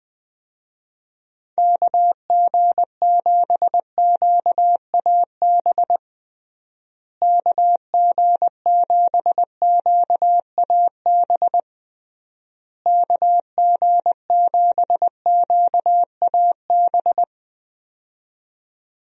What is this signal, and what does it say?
Morse code, heard as a keyed tone, about 20 wpm, 700 Hz. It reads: KG7QAB KG7QAB KG7QAB
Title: USA Call Signs - 6 Characters - Repeated 3x - 20wpm